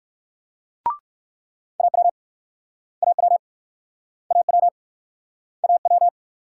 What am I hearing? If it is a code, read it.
Morse code, heard as a keyed tone, about 56 wpm, 700 Hz. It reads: AW AW AW AW
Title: Speed-Racing Multi-letter Phonemes - 45wpm